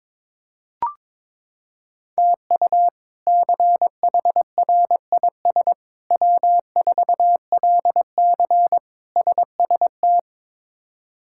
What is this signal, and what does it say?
Morse code, heard as a keyed tone, about 22 wpm, 700 Hz. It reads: TU CHRIS W4LC SST
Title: K1USN SST QSO Practice - 22wpm